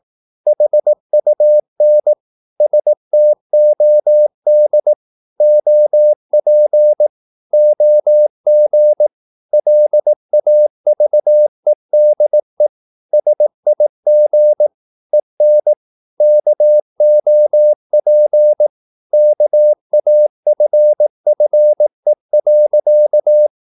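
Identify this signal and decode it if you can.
Morse code, heard as a keyed tone, about 18 wpm, 600 Hz. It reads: HUN STOD OP OG LAVEDE SIG EN KOP KAFFE.
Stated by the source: Slow (QRS) Morse practice at 18 wpm, CW med Gnister.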